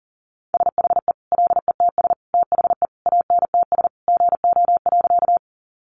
Morse code, heard as a keyed tone, about 40 wpm, 700 Hz. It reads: SHE LETS THE ANTS GO.
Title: Additional Easy Phrases 40wpm